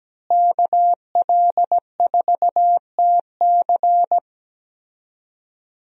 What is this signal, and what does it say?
Morse code, heard as a keyed tone, about 17 wpm, 700 Hz. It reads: KL4TC
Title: USA Call Signs - 5 Characters - Repeated 3x - 17wpm